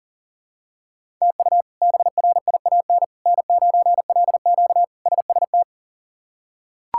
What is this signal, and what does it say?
Morse code, heard as a keyed tone, about 40 wpm, 700 Hz. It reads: TU BRIAN N9LQ SST
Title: K1USN SST QSO Practice - 40wpm